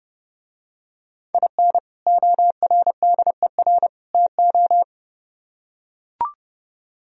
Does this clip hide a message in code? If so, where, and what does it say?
Yes: Morse code from 1.3 to 4.8 s, heard as a keyed tone, about 29 wpm, 700 Hz. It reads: IN ORDER TO